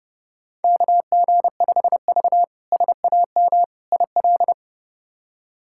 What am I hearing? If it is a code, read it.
Morse code, heard as a keyed tone, about 30 wpm, 700 Hz. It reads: KG5V SAM IL
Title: K1USN SST QSO Practice - 30wpm